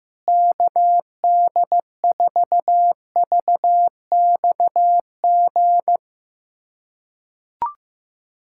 Morse code, heard as a keyed tone, about 15 wpm, 700 Hz. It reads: KD4VXG
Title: USA Callsign DE Callsign 15wpm